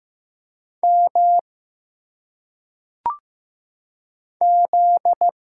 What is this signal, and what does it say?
Morse code, heard as a keyed tone, about 15 wpm, 700 Hz. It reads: M Z
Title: Single Letters - Phonetic Pronunciation 15wpm